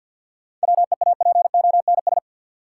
Morse code, heard as a keyed tone, about 52 wpm, 700 Hz. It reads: WEAPONS
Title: Sets of 5 Words Encore - New Word Drill - 50wpm